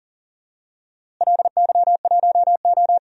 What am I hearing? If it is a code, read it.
Morse code, heard as a keyed tone, about 40 wpm, 700 Hz. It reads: LY1O
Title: DXCC-100 Call Signs - 4 Characters - Repeated 3x - 40wpm